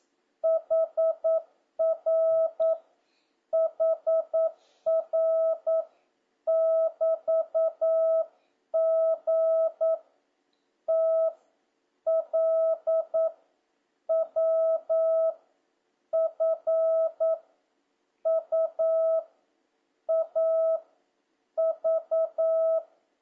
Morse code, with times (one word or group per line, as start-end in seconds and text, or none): 0.4-2.7 HR
3.5-5.8 HR
6.5-9.9 <BT>G
10.9-11.3 T
12.1-13.3 L
14.1-15.3 W
16.1-17.3 F
18.3-19.2 U
20.1-20.8 A
21.6-22.8 V